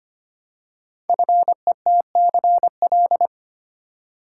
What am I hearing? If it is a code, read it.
Morse code, heard as a keyed tone, about 25 wpm, 700 Hz. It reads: FETCL
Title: Code Groups - 5 Letters - 25wpm